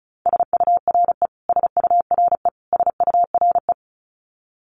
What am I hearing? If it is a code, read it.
Morse code, heard as a keyed tone, about 34 wpm, 700 Hz. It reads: SURE SURE SURE
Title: Top 500 Words Repeated 35wpm